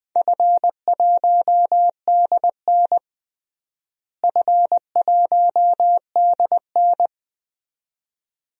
Morse code, heard as a keyed tone, about 20 wpm, 700 Hz. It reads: F1DN F1DN
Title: DXCC-100 Call Signs - 4 Characters - Repeated 2x - 20wpm